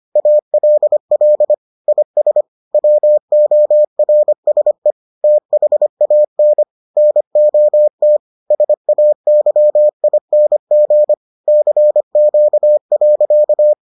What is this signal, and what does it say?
Morse code, heard as a keyed tone, about 25 wpm, 600 Hz. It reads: ALL IS WORSE THAN NOT SAYING CQ.